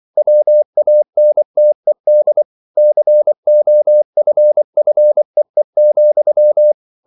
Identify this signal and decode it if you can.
Morse code, heard as a keyed tone, about 24 wpm, 600 Hz. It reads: WANTED COFFEE,